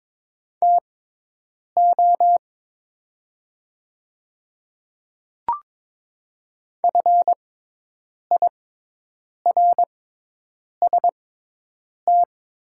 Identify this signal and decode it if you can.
Morse code, heard as a keyed tone, about 23 wpm, spaced as for 8 wpm, 700 Hz. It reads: TO FIRST
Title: 6x Character Spacing - Top 100 Words - 22wpm